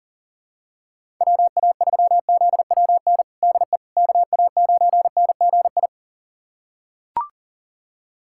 Morse code, heard as a keyed tone, about 40 wpm, 700 Hz. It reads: WA3ZWN DE KA9NGI
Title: USA Callsign DE Callsign 40wpm